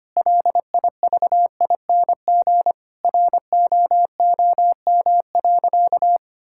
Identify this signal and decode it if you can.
Morse code, heard as a keyed tone, about 25 wpm, 700 Hz. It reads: LIVING ROOM.
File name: practice / 25wpm / Sentences with 4-Word Phrases - 25wpm